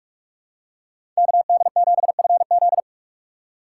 Morse code, heard as a keyed tone, about 42 wpm, 700 Hz. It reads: KD7FZ
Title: USA Call Signs - 5 Characters - 3x Character Spacing - 45wpm